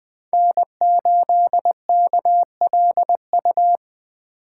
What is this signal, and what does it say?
Morse code, heard as a keyed tone, about 20 wpm, 700 Hz. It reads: N8KLU
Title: USA Call Signs - 5 Characters - 2x Character Spacing - 20wpm